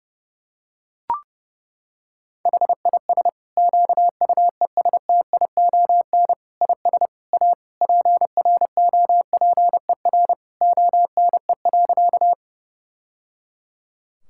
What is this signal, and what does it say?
Morse code, heard as a keyed tone, about 31 wpm, 700 Hz. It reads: HIS QUESTION IS A PROPER ONE.
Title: Sentences from Top 1000 Words - Review Part 5 - 30wpm